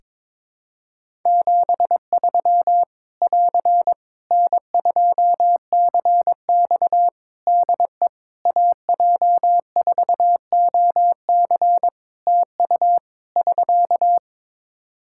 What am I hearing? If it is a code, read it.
Morse code, heard as a keyed tone, about 22 wpm, 700 Hz. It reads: 73 <AR> N2CX DE AJ4OC TU <SK>
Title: Practice QSO - Exchange 4 - Ending - 22wpm